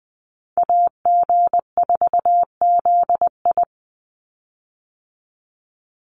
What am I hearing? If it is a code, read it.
Morse code, heard as a keyed tone, about 20 wpm, 700 Hz. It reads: AG4ZI
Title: USA Call Signs 20wpm